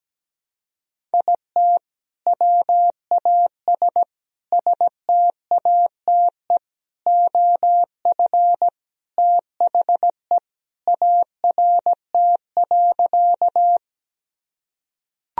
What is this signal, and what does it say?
Morse code, heard as a keyed tone, about 17 wpm, 700 Hz. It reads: IT WAS STATE OF THE ART.
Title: Sentences from Top 1000 Words - Review Part 3 - 17wpm